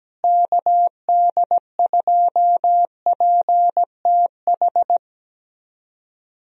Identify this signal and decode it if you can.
Morse code, heard as a keyed tone, about 17 wpm, 700 Hz. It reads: KD2PTH
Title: USA Call Signs - 6 Characters - Repeated 3x - 17wpm